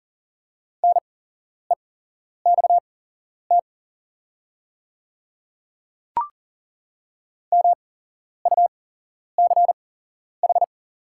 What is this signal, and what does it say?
Morse code, heard as a keyed tone, about 40 wpm, spaced as for 11 wpm, 690 Hz. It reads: NEXT MUCH
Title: 8x Character Spacing - Top 500 Words - 40wpm